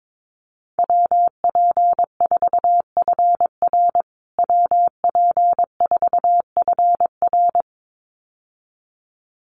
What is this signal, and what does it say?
Morse code, heard as a keyed tone, about 22 wpm, 700 Hz. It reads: WP4FR WP4FR
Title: USA Call Signs Repeated 2x 22wpm